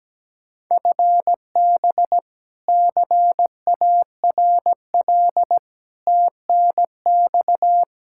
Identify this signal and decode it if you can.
Morse code, heard as a keyed tone, about 17 wpm, 700 Hz. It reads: FB CARL TNX
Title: Practice QSO - Exchange 3 - More Info - 17wpm